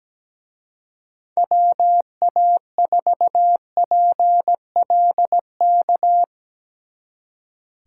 Morse code, heard as a keyed tone, about 17 wpm, 700 Hz. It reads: WA4PLK
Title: Speed Racing - USA 2x3 Call Signs 17wpm